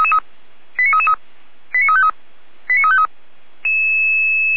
Five-tone selective call (ZVEI), heard as quick runs of five tones, tones of about 70 ms. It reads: 89333 89353 89353